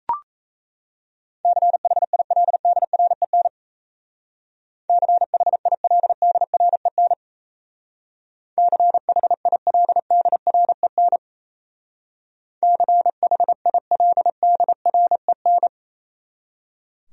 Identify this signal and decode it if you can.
Morse code, heard as a keyed tone, about 34 wpm, 700 Hz. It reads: CHILDREN CHILDREN CHILDREN CHILDREN
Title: Speed-Racing Top 300 Words 28wpm